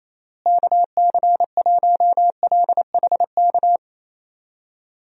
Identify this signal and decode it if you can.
Morse code, heard as a keyed tone, about 28 wpm, 700 Hz. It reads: KC1LHK
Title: USA Call Signs - 6 Characters - Repeated 3x - 28wpm